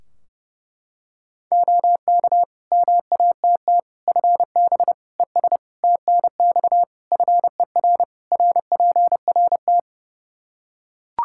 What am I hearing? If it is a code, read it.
Morse code, heard as a keyed tone, about 30 wpm, 700 Hz. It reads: OK MATT FB ES TNX FER RPRT
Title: Practice QSO - Exchange 2 - Info - 30wpm